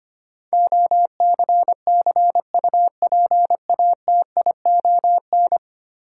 Morse code, heard as a keyed tone, about 25 wpm, 700 Hz. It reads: OCCUPATION
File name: Ten Letter Words - 25wpm